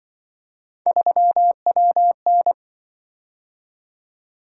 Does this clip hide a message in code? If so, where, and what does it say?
Yes: Morse code from 0.9 to 2.5 s, heard as a keyed tone, about 25 wpm, 700 Hz. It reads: 3WN